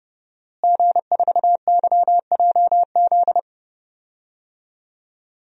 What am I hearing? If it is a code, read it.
Morse code, heard as a keyed tone, about 30 wpm, 700 Hz. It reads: G4YJZ
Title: DXCC-100 Call Signs - 5 Characters - Repeated 3x - 30wpm